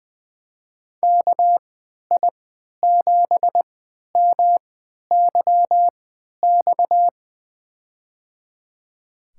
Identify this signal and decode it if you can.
Morse code, heard as a keyed tone, about 20 wpm, spaced as for 11 wpm, 700 Hz. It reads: KI7MYX
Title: USA Call Signs - 6 Characters - 3x Character Spacing - 20wpm